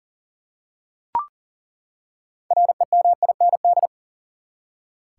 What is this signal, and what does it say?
Morse code, heard as a keyed tone, about 40 wpm, 700 Hz. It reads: REMIND
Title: Six Letter Words - 40wpm